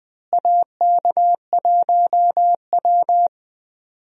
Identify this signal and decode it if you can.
Morse code, heard as a keyed tone, about 20 wpm, 700 Hz. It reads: AK1W
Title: USA Call Signs - 4 Characters - Repeated 3x - 20wpm